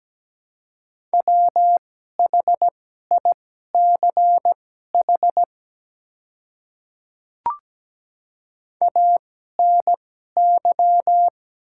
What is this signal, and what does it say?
Morse code, heard as a keyed tone, about 17 wpm, spaced as for 12 wpm, 700 Hz. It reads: WHICH ANY
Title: 2x Character Spacing - Top 300 Words - 17wpm